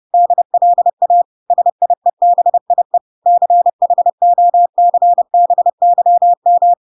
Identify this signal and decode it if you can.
Morse code, heard as a keyed tone, about 30 wpm, 700 Hz. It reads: DLA SIEBIE CHOCBYM